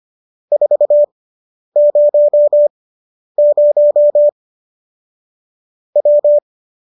Morse code, heard as a keyed tone, about 25 wpm, spaced as for 10 wpm, 590 Hz. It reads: 400 W